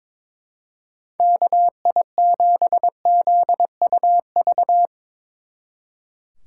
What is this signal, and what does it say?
Morse code, heard as a keyed tone, about 24 wpm, 700 Hz. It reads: KI7ZUV
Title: Speed Racing - USA 2x3 Call Signs 22wpm